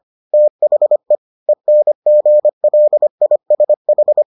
Morse code, heard as a keyed tone, about 25 wpm, 600 Hz. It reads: THE ENGLISH